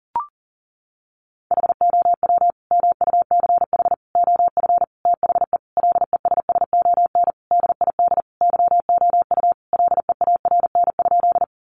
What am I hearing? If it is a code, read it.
Morse code, heard as a keyed tone, about 37 wpm, 700 Hz. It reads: HOW MUCH OF THE LESSON DID YOU LEARN?
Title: Sentences from Top 1200 Words 40wpm